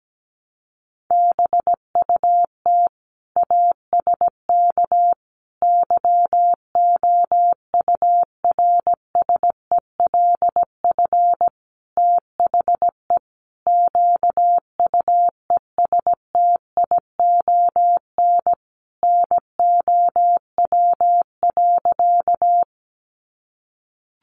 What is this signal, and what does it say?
Morse code, heard as a keyed tone, about 17 wpm, 700 Hz. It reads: BUT ASK YOURSELF THE QUESTION NOW.
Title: Sentences from Top 600 Words 17wpm